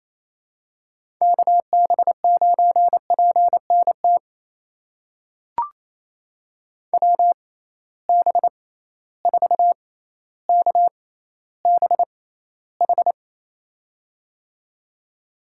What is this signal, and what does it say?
Morse code, heard as a keyed tone, about 28 wpm, spaced as for 14 wpm, 700 Hz. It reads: KB9PNT WB4KBH
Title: USA Call Signs - 6 Characters - 6x Character Spacing - 28wpm